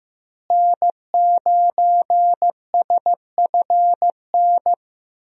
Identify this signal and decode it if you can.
Morse code, heard as a keyed tone, about 15 wpm, 700 Hz. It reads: N9SFN
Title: USA Call Signs - 5 Characters - Repeated 3x - 15wpm